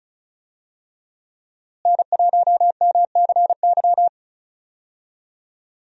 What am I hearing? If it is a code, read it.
Morse code, heard as a keyed tone, about 34 wpm, 700 Hz. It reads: N1MCY
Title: USA Call Signs - 5 Characters - Repeated 3x - 35wpm